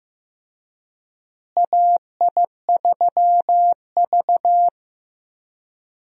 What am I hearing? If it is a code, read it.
Morse code, heard as a keyed tone, about 15 wpm, 700 Hz. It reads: AI3V
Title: USA Call Signs - 4 Characters - Repeated 2x - 15wpm